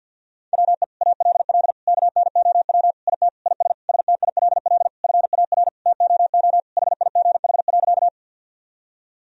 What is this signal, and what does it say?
Morse code, heard as a keyed tone, about 50 wpm, 700 Hz. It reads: WE ALL KNOW IT IS STILL FAR TOO HIGH.
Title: Sentences from Top 1000 Words - Review Part 1 - 50wpm